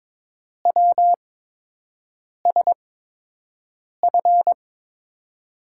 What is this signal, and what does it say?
Morse code, heard as a keyed tone, about 23 wpm, spaced as for 6 wpm, 700 Hz. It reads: WSF
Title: USA Call Signs - 8x Character Spacing - 22wpm